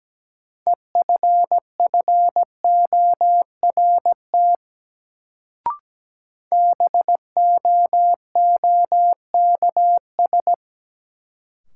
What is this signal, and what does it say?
Morse code, heard as a keyed tone, about 17 wpm, 700 Hz. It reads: EFFORT BOOKS
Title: Sentences from Top 1200 Words - New Word Drill Alt - 17wpm